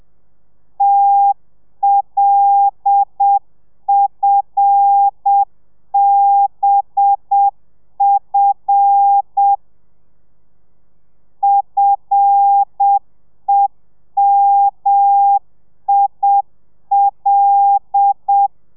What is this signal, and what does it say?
Morse code, heard as a keyed tone, about 7 wpm, 800 Hz. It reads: TLFBF FEMIL
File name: telegrafia / i2tfj / Lez5 30